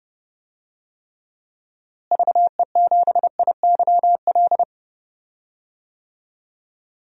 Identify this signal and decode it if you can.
Morse code, heard as a keyed tone, about 32 wpm, 700 Hz. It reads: VE7IYL
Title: DXCC-100 Call Signs - 6 Characters - Repeated 3x - 30wpm